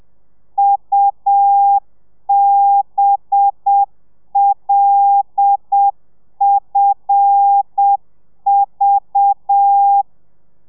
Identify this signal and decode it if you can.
Morse code, heard as a keyed tone, about 7 wpm, 800 Hz. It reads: UBLFV
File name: telegrafia / i2tfj / Lez5 30